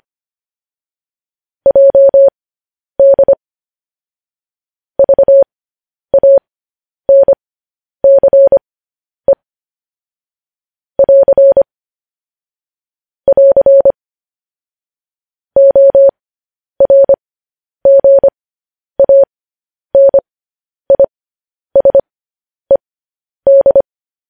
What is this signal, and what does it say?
Morse code, heard as a keyed tone, about 25 wpm, spaced as for 10 wpm, 550 Hz. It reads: JD VANCE <AR> <AR> ORGANISED
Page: news headlines in morse code at 10 wpm